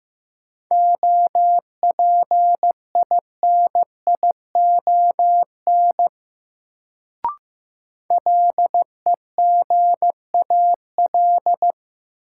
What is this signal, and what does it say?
Morse code, heard as a keyed tone, about 15 wpm, 700 Hz. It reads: OPINION LEGAL
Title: Sentences from Top 1200 Words - New Word Drill - 15wpm